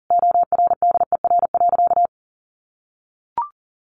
Morse code, heard as a keyed tone, about 40 wpm, 700 Hz. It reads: ORDER.
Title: Sentences from Top 1100 Words 40wpm